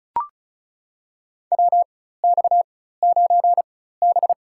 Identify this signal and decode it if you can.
Morse code, heard as a keyed tone, about 36 wpm, spaced as for 16 wpm, 700 Hz. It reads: WX9B
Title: USA Call Signs - 4 Characters - 4x Character Spacing - 35wpm